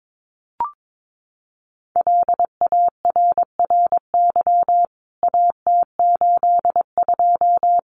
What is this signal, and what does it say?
Morse code, heard as a keyed tone, about 22 wpm, 700 Hz. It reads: LARRY AT82
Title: CWT Exchange 22wpm